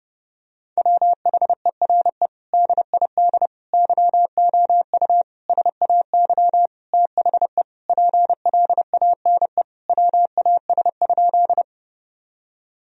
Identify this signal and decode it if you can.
Morse code, heard as a keyed tone, about 32 wpm, 700 Hz. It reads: WHERE DID YOU SAY THE PLANE WAS?
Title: Sentences from Top 1000 Words - Review Part 3 - 30wpm